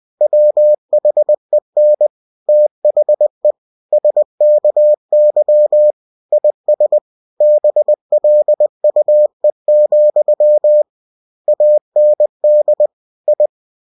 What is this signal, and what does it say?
Morse code, heard as a keyed tone, about 20 wpm, 600 Hz. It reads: WHEN THE SKY IS BLUE, AND I
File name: cw-books / winnie-the-pooh / winnie-20wpm-page-0068